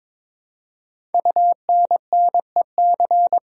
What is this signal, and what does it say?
Morse code, heard as a keyed tone, about 22 wpm, 700 Hz. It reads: UNNEC
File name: practice / 22wpm / Eleven Letter Words - 22wpm